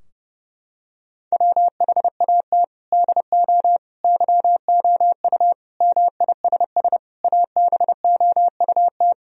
Morse code, heard as a keyed tone, about 30 wpm, 700 Hz. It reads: WHAT DO YOU MISS ABOUT
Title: Sentences from Top 1000 Words - Review Part 3 - 30wpm